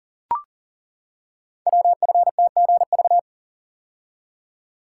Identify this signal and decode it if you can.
Morse code, heard as a keyed tone, about 42 wpm, 700 Hz. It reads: WFTGV
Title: Code Groups - 5 Letters - 40wpm